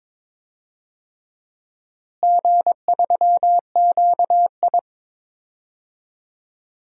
Morse code, heard as a keyed tone, about 22 wpm, 700 Hz. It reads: G3QI
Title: DXCC-100 Call Signs - 4 Characters - Repeated 3x - 22wpm